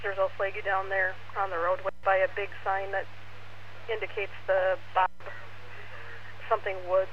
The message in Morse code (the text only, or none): none